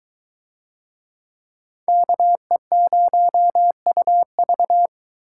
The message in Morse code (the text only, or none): KE0UV